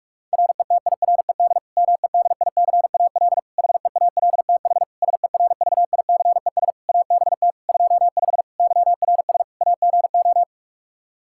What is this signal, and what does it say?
Morse code, heard as a keyed tone, about 46 wpm, 700 Hz. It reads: RETIRED MEDICAL HEALTH SERVICES ABT 25 YRS AGO
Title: Practice QSO - Exchange 3 - More Info - 45wpm